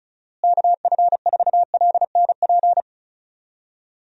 Morse code, heard as a keyed tone, about 36 wpm, 700 Hz. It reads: KF4LNP
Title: USA Call Signs - 6 Characters - Repeated 3x - 35wpm